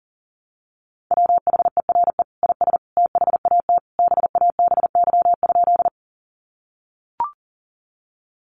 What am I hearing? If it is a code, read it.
Morse code, heard as a keyed tone, about 40 wpm, 700 Hz. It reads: WHERE IS THAT BABY?
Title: Sentences from Top 1300 Words 40wpm